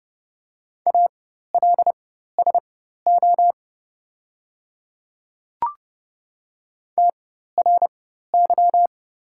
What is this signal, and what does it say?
Morse code, heard as a keyed tone, about 29 wpm, spaced as for 14 wpm, 700 Hz. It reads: ALSO TRY
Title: 4x Character Spacing - Top 500 Words - 30wpm